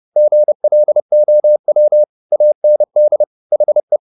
Morse code, heard as a keyed tone, about 30 wpm, 600 Hz. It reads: GLOW AND HE